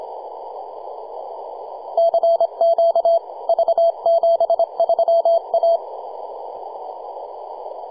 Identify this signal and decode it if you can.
Morse code, heard as a keyed tone, about 25 wpm, 670 Hz. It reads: CQ V73A
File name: posts / 2015-sept-26-2040z-v73a-21-025-cw